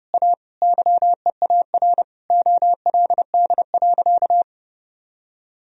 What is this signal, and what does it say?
Morse code, heard as a keyed tone, about 30 wpm, 700 Hz. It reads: A YEAR OLD.
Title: Sentences from Top 1000 Words - Review Part 2 - 30wpm